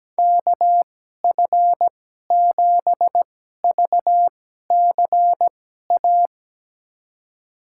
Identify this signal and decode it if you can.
Morse code, heard as a keyed tone, about 17 wpm, spaced as for 12 wpm, 700 Hz. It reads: KF7VCA
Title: USA Call Signs - 6 Characters - 2x Character Spacing - 17wpm